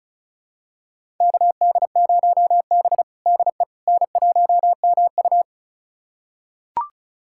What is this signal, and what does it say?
Morse code, heard as a keyed tone, about 34 wpm, 700 Hz. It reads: KD0B DE N1MU
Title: USA Callsign DE Callsign - 4 Characters - 35wpm